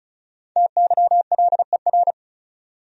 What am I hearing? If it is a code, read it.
Morse code, heard as a keyed tone, about 36 wpm, 700 Hz. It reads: TYLER
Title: USA Common Names - Max 5 Letters - 35wpm